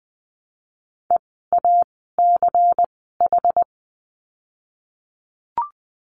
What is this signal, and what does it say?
Morse code, heard as a keyed tone, about 20 wpm, 700 Hz. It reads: E A C H
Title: 206 Lesson 33f - Letter j - Words 20wpm